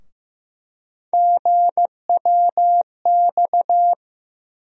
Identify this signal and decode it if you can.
Morse code, heard as a keyed tone, about 15 wpm, 700 Hz. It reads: GWX
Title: USA Call Sign Postfixes - 3 Letters 15wpm